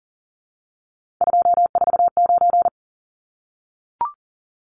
Morse code, heard as a keyed tone, about 40 wpm, 700 Hz. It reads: 249
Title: Numbers - 3-Digit 40wpm